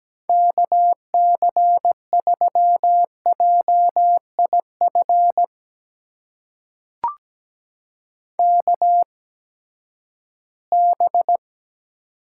Morse code, heard as a keyed tone, about 17 wpm, spaced as for 12 wpm, 700 Hz. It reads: KC3JIF KB
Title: USA Call Signs - 6 Characters - 8x Character Spacing - 17wpm